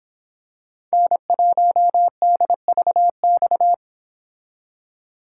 Morse code, heard as a keyed tone, about 26 wpm, 700 Hz. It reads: N1DVX